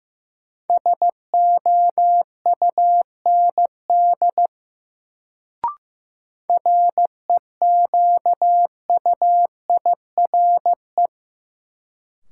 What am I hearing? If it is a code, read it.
Morse code, heard as a keyed tone, about 15 wpm, 700 Hz. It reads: SOUND REQUIRE
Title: Sentences from Top 900 Words - New Word Drill - 15wpm